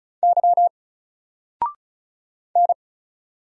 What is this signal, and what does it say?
Morse code, heard as a keyed tone, about 34 wpm, 700 Hz. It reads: Y N
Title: Single Letter-Number 35wpm